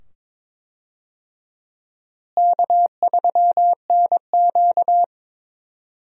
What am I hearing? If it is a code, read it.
Morse code, heard as a keyed tone, about 22 wpm, 700 Hz. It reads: K3NQ
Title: USA Call Signs - 4 Characters - Repeated 2x - 22wpm